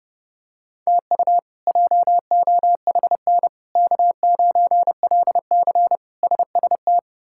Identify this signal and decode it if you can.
Morse code, heard as a keyed tone, about 30 wpm, 700 Hz. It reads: TU JOHN K9LC SST
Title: K1USN SST QSO Practice - 30wpm